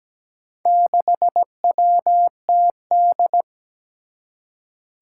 Morse code, heard as a keyed tone, about 17 wpm, 700 Hz. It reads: 6WTD
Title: Four Challenging Letter-Number Combinations 17wpm